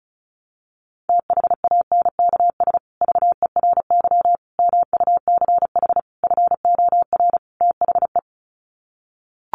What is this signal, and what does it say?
Morse code, heard as a keyed tone, about 34 wpm, 700 Hz. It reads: THANKS VERY MUCH FOR THE
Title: Sets of 5 Words Encore2 35wpm